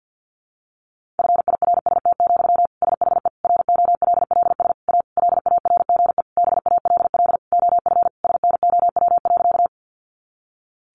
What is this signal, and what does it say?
Morse code, heard as a keyed tone, about 50 wpm, 700 Hz. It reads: FIRST, SHE ROLLS A LARGE BALL OF SNOW.
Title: Easy Phrases 50wpm